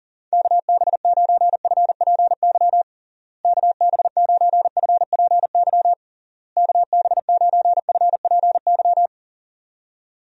Morse code, heard as a keyed tone, about 40 wpm, 700 Hz. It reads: KB9FPY KB9FPY KB9FPY
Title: USA Call Signs - 6 Characters - Repeated 3x - 40wpm